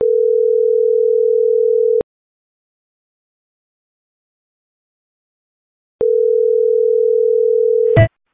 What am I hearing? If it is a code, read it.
Ringback tone (North America), 440 + 480 Hz.